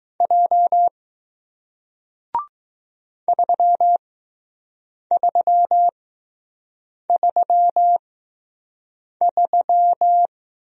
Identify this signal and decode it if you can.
Morse code, heard as a keyed tone, about 20 wpm, spaced as for 6 wpm, 700 Hz. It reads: J 3333